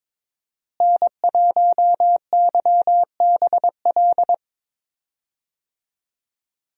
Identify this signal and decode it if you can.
Morse code, heard as a keyed tone, about 22 wpm, 700 Hz. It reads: N1YBL
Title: USA Call Signs - 5 Characters - Repeated 2x - 22wpm